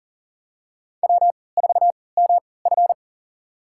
Morse code, heard as a keyed tone, about 40 wpm, spaced as for 23 wpm, 700 Hz. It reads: W4MF
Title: USA Call Signs - 4 Characters - 3x Character Spacing - 40wpm